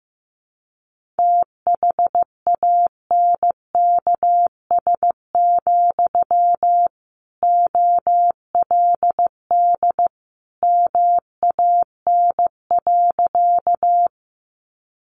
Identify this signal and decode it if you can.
Morse code, heard as a keyed tone, about 15 wpm, 700 Hz. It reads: THANKS, OLD MAN.
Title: Sentences from Top 600 Words 15wpm